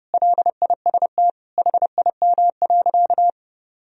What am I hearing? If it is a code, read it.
Morse code, heard as a keyed tone, about 30 wpm, 700 Hz. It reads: LIST HIM.
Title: Sentences from Top 1000 Words - Review Part 2 - 30wpm